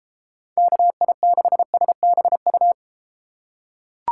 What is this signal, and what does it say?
Morse code, heard as a keyed tone, about 32 wpm, 700 Hz. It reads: KI6SBU